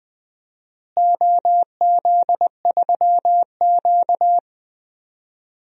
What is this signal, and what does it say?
Morse code, heard as a keyed tone, about 20 wpm, 700 Hz. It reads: OZ3Q
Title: DXCC-100 Call Signs - 4 Characters - Repeated 3x - 20wpm